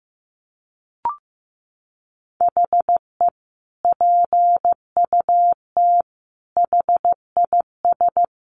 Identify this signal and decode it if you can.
Morse code, heard as a keyed tone, about 15 wpm, 700 Hz. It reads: HE PUT HIS
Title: Sentences from Top 1200 Words 15wpm